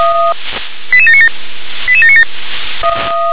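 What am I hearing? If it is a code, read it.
Five-tone selective call (ZVEI), heard as quick runs of five tones, tones of about 70 ms. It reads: 88797 88797
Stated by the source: ZVEI five-tone selective call received by SDR.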